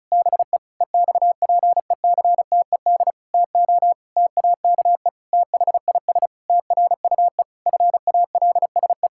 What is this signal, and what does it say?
Morse code, heard as a keyed tone, about 36 wpm, 700 Hz. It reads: BE EXPECTED TO TAKE THIS TRUE FALSE
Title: Pangram Challenge v2 - 35wpm